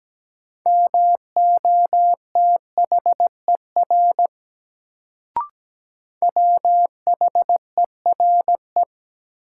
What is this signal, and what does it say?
Morse code, heard as a keyed tone, about 17 wpm, 700 Hz. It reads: MOTHER WHERE
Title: Sentences from Top 200 Words - New Word Drill - 17wpm